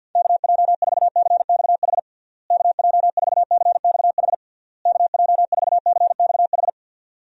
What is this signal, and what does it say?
Morse code, heard as a keyed tone, about 47 wpm, 700 Hz. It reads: KJ4CXH KJ4CXH KJ4CXH
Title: USA Call Signs - 6 Characters - Repeated 3x - 50wpm